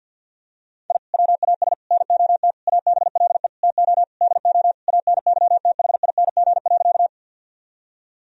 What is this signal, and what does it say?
Morse code, heard as a keyed tone, about 49 wpm, 700 Hz. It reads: I WAS NOT ABLE TO DO ANYTHING.
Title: Sentences from Top 1000 Words - Review Part 2 - 50wpm